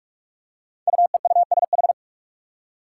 Morse code, heard as a keyed tone, about 46 wpm, 700 Hz. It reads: UEUSH
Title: Five Challenging Letter-Number Combinations 45wpm